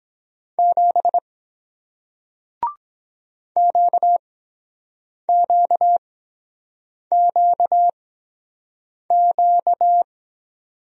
Morse code, heard as a keyed tone, about 23 wpm, spaced as for 7 wpm, 700 Hz. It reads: 7 QQQQ